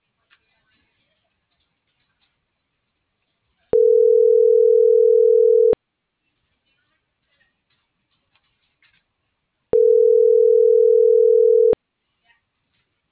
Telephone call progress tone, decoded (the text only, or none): ringback tone